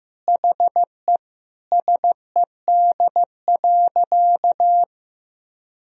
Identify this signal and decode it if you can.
Morse code, heard as a keyed tone, about 15 wpm, 700 Hz. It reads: HE SED.
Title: Sentences from Top 200 Words and Abbreviations 15wpm